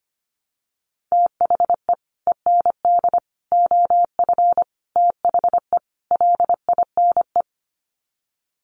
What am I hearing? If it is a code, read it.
Morse code, heard as a keyed tone, about 25 wpm, 700 Hz. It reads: THE END OF THE LINE
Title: Sets of 5 Words Encore2 25wpm